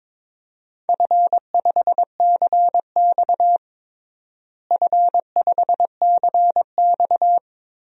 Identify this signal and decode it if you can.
Morse code, heard as a keyed tone, about 22 wpm, 700 Hz. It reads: F5CX F5CX
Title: DXCC-100 Call Signs - 4 Characters - Repeated 2x - 22wpm